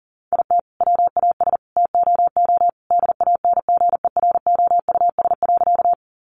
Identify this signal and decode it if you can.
Morse code, heard as a keyed tone, about 40 wpm, 700 Hz. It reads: IT WAS TOO DANGEROUS.
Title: Sentences from Top 1200 Words 40wpm